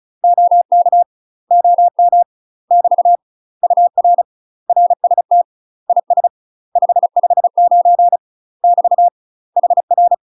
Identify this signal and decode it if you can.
Morse code, heard as a keyed tone, about 34 wpm, 700 Hz. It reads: OK OM <BT> UR RST IS 559 <BT> HR
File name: UK MASTER QSOs TRAINING / 35WPM / UK MASTER QSOs 0006